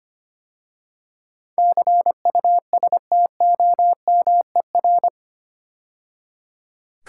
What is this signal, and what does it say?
Morse code, heard as a keyed tone, about 25 wpm, 700 Hz. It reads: CUSTOMER
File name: 8 Letter Words - First 3 Letters Given - 25wpm